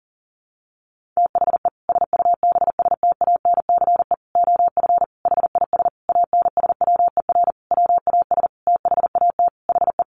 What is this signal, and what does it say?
Morse code, heard as a keyed tone, about 40 wpm, 700 Hz. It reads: THE SUBSTANCE OF HIS ANSWER WAS THAT HE
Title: Sentences from Top 1500 Words 40wpm